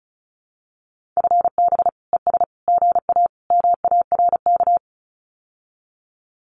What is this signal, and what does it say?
Morse code, heard as a keyed tone, about 34 wpm, 700 Hz. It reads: FB ES GA MARK
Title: Common QSO Phrases Encore 35wpm